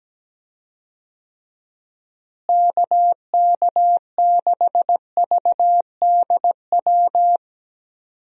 Morse code, heard as a keyed tone, about 17 wpm, 700 Hz. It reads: KK6VDW
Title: USA Call Signs - 6 Characters - Repeated 3x - 17wpm